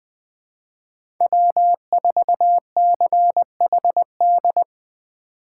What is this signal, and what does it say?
Morse code, heard as a keyed tone, about 20 wpm, 700 Hz. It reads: W4CHD
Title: USA Call Signs - 5 Characters - 3x Character Spacing - 20wpm